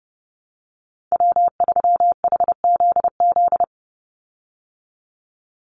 Morse code, heard as a keyed tone, about 30 wpm, 700 Hz. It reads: W3HZZ
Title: Call Signs Repeated 3x - 30wpm